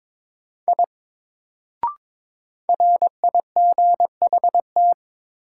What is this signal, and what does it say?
Morse code, heard as a keyed tone, about 22 wpm, 700 Hz. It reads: I RIGHT
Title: Sentences from Top 200 Words - New Word Drill - 22wpm